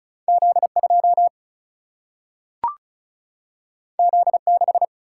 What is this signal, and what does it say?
Morse code, heard as a keyed tone, about 36 wpm, 700 Hz. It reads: Z2 Z6
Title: Single-Letter Single-Number 35wpm